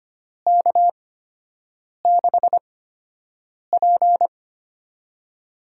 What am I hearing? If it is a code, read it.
Morse code, heard as a keyed tone, about 25 wpm, spaced as for 7 wpm, 700 Hz. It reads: K6P
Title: USA Call Signs - 4 Characters - 8x Character Spacing - 25wpm